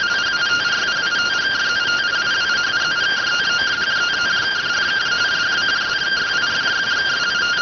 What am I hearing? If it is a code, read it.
Two-tone teleprinter signal, about 170 Hz shift.